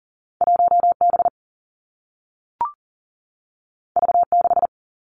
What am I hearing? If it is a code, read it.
Morse code, heard as a keyed tone, about 39 wpm, 700 Hz. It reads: 1 B V 6
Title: Two Challenging Letter-Number Combinations 40wpm